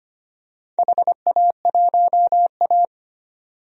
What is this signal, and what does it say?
Morse code, heard as a keyed tone, about 25 wpm, 700 Hz. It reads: HA1A
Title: DXCC-100 Call Signs - 4 Characters - 25wpm